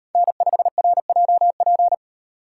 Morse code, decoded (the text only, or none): N5RJP